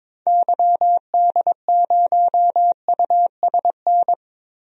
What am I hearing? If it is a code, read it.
Morse code, heard as a keyed tone, about 22 wpm, 700 Hz. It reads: YD0USN
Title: DXCC-100 Call Signs - 6 Characters - Repeated 3x - 22wpm